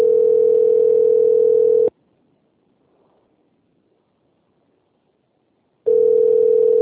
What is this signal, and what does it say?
Ringback tone (North America), 440 + 480 Hz.
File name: rg-1600-unknown- - - .82570